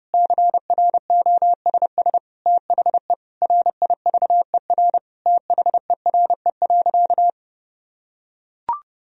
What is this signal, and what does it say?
Morse code, heard as a keyed tone, about 30 wpm, 700 Hz. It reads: CROSS THE RIVER THERE.
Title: Sentences from Top 1000 Words - Review Part 3 - 30wpm